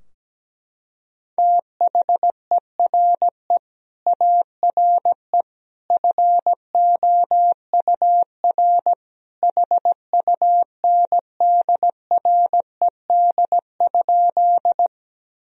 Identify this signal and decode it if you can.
Morse code, heard as a keyed tone, about 17 wpm, 700 Hz. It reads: THERE ARE FOUR HUNDRED?
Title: Sentences from Top 1000 Words - Review Part 2 - 17wpm